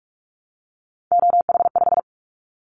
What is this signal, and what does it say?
Morse code, heard as a keyed tone, about 44 wpm, 700 Hz. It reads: OH5
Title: Callsign Prefixes for Top 50 DXCC - 45wpm